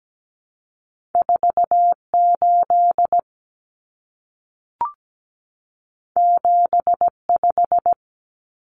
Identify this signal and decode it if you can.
Morse code, heard as a keyed tone, about 17 wpm, 700 Hz. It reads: 48 75
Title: Common Cut Numbers - 2-Digit 17wpm